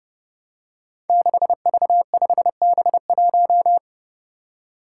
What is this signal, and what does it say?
Morse code, heard as a keyed tone, about 30 wpm, 700 Hz. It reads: 6V5B1
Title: Five Challenging Letter-Number Combinations 30wpm